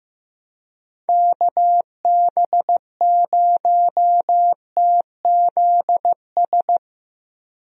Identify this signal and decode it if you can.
Morse code, heard as a keyed tone, about 15 wpm, 700 Hz. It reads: KB0TZS
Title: USA Call Signs - 6 Characters - Repeated 3x - 15wpm